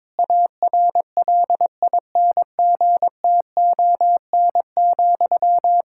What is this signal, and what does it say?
Morse code, heard as a keyed tone, about 22 wpm, 700 Hz. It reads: ARLINGTON,
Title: K1USN SST QSO Practice 22wpm